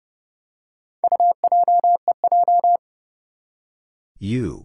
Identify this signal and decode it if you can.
Morse code, heard as a keyed tone, about 30 wpm, 700 Hz. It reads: UJEJ
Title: Four Challenging Letter-Number Combinations 30wpm